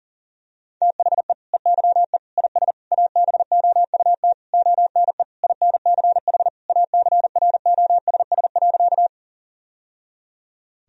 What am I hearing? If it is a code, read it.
Morse code, heard as a keyed tone, about 37 wpm, 700 Hz. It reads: THE EYE IS ABOUT ONE INCH ACROSS.
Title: Sentences from Top 1000 Words - Review Part 3 - 40wpm